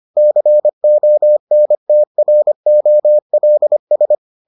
Morse code, heard as a keyed tone, about 25 wpm, 600 Hz. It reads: CONTROLS